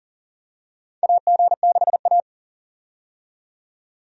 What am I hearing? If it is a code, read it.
Morse code, heard as a keyed tone, about 40 wpm, 700 Hz. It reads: AG6A
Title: USA Call Signs - 4 Characters - 40wpm